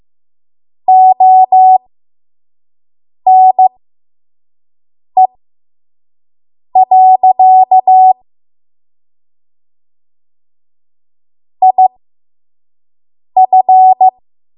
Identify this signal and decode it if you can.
Morse code, heard as a keyed tone, about 15 wpm, spaced as for 5 wpm, 750 Hz. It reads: ONE. IF
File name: Morse / 05WPM